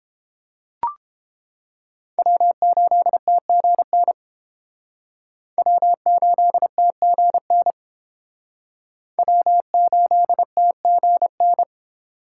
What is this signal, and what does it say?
Morse code, heard as a keyed tone, about 28 wpm, 700 Hz. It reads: W8TGN W8TGN W8TGN